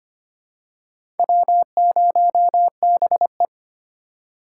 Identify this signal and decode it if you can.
Morse code, heard as a keyed tone, about 25 wpm, 700 Hz. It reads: W0BE
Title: USA Call Signs - 4 Characters - 2x Character Spacing - 25wpm